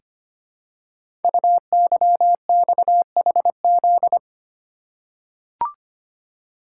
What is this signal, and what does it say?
Morse code, heard as a keyed tone, about 25 wpm, 700 Hz. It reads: UYXHZ
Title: Code Groups - 5 Letters - 25wpm